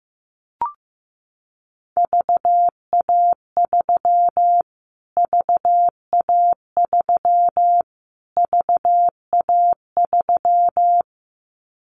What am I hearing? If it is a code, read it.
Morse code, heard as a keyed tone, about 15 wpm, 700 Hz. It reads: VA3 VA3 VA3
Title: Callsign Prefixes for Top 25 DXCC - Repeated 3x - 15wpm